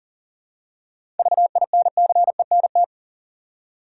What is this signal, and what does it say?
Morse code, heard as a keyed tone, about 40 wpm, 700 Hz. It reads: VINCENT
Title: USA Common Names - 40wpm